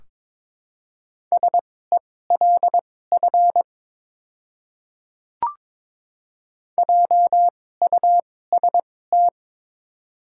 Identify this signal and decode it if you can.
Morse code, heard as a keyed tone, about 22 wpm, spaced as for 16 wpm, 700 Hz. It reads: SELF JUST
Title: 2x Character Spacing - Top 200 Words - 22wpm